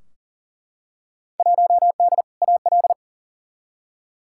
Morse code, heard as a keyed tone, about 40 wpm, 700 Hz. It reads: 1D AL
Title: Field Day Exchanges - 40wpm